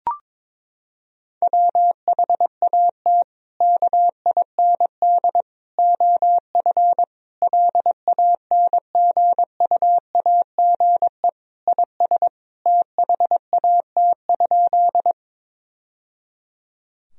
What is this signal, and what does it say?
Morse code, heard as a keyed tone, about 22 wpm, 700 Hz. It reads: WHAT KIND OF LANGUAGE IS THAT?
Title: Sentences from Top 1000 Words - Review Part 3 - 22wpm